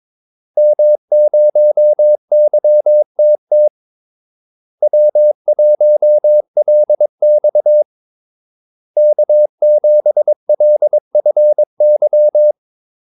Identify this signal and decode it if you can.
Morse code, heard as a keyed tone, about 22 wpm, 600 Hz. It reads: M0YTT W1LX K7LFY
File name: Callsigns-22wpm